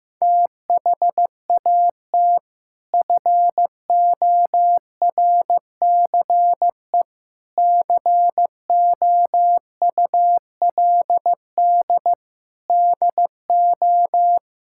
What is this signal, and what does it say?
Morse code, heard as a keyed tone, about 15 wpm, 700 Hz. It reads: THAT FORCE COULD DO
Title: Sentences from Top 1000 Words - Review Part 5 - 15wpm